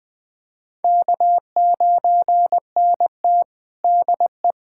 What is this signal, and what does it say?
Morse code, heard as a keyed tone, about 21 wpm, 700 Hz. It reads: K9NT DE
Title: USA Callsign DE Callsign - 4 Characters - 20wpm